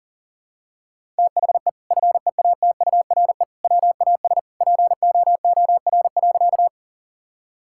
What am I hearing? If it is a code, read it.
Morse code, heard as a keyed tone, about 40 wpm, 700 Hz. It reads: THE FEATURE WAS POOR.
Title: Sentences from Top 1000 Words - Review Part 5 - 40wpm